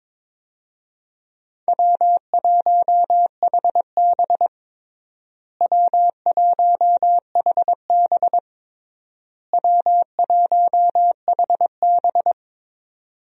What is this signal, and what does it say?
Morse code, heard as a keyed tone, about 22 wpm, 700 Hz. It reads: W1HB W1HB W1HB
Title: USA Call Signs - 4 Characters - Repeated 3x - 22wpm